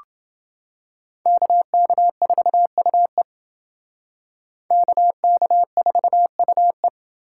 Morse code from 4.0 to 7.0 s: KK4UE

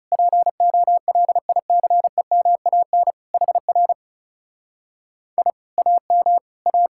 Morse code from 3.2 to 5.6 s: HR I

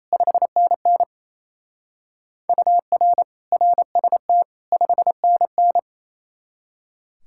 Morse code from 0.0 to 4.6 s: 5NN UR RST

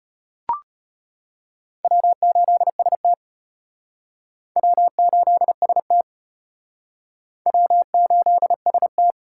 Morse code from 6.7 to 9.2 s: W8ST

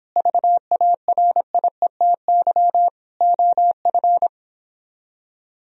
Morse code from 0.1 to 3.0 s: VARIETY

